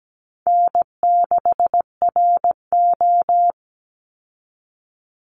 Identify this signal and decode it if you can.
Morse code, heard as a keyed tone, about 17 wpm, 700 Hz. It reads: N6RO
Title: Easy Call Signs Repeated 2x - 17wpm